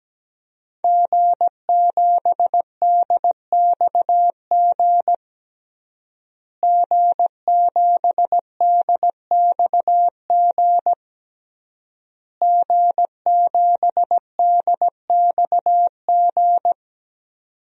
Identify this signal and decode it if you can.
Morse code, heard as a keyed tone, about 17 wpm, 700 Hz. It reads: G7DXG G7DXG G7DXG
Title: DXCC-100 Call Signs - 5 Characters - Repeated 3x - 17wpm